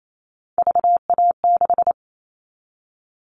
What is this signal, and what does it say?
Morse code, heard as a keyed tone, about 28 wpm, 700 Hz. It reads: VA6
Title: Callsign Prefixes for Top 100 DXCC - 28wpm